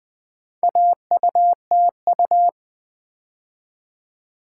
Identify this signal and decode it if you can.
Morse code, heard as a keyed tone, about 21 wpm, 700 Hz. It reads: AUTU
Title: Build A Word - 2 Letters At A Time - 6-Letter Words - 20wpm